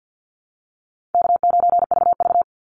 Morse code, heard as a keyed tone, about 48 wpm, 700 Hz. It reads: K9VV
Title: Easy Call Signs Repeated 3x - 50wpm